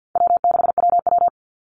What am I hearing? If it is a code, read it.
Morse code, heard as a keyed tone, about 49 wpm, 700 Hz. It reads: W6WW
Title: NA SOTA Chaser Callsigns 50wpm